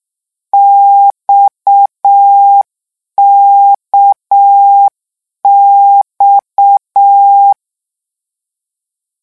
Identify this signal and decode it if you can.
Morse code, heard as a keyed tone, about 6 wpm, 800 Hz. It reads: XKX
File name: morse mp3 / lecon 1-vitesse 6